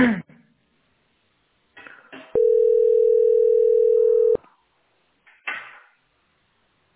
Ringback tone (North America), 440 + 480 Hz.